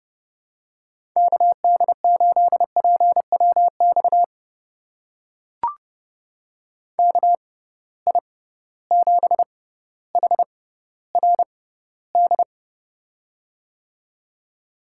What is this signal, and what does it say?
Morse code, heard as a keyed tone, about 30 wpm, spaced as for 15 wpm, 700 Hz. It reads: KD8PWX KI7HRD
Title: USA Call Signs - 6 Characters - 6x Character Spacing - 30wpm